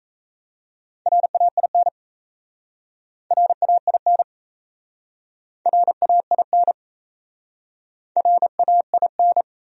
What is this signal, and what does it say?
Morse code, heard as a keyed tone, about 37 wpm, 700 Hz. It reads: RAIN RAIN RAIN RAIN